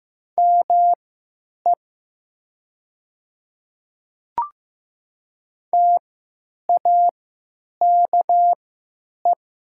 Morse code, heard as a keyed tone, about 15 wpm, spaced as for 9 wpm, 700 Hz. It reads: ME TAKE
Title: 3x Character Spacing - Top 500 Words - 15wpm